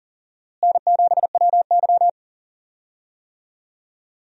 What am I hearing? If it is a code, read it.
Morse code, heard as a keyed tone, about 40 wpm, 700 Hz. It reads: N7WY
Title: USA Call Signs - 4 Characters - Repeated 3x - 40wpm